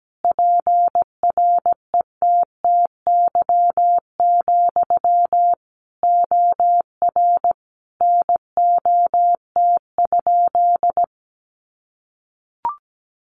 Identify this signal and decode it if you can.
Morse code, heard as a keyed tone, about 17 wpm, 700 Hz. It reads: PRETTY, OR NOT?
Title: Sentences from Top 1000 Words 17wpm